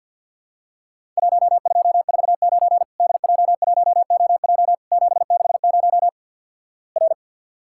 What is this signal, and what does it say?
Morse code, heard as a keyed tone, about 50 wpm, 700 Hz. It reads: 1249 DJ1OJ 760 R
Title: Worked All Europe DX Contest - QTC Exchange with Repeats - 50wpm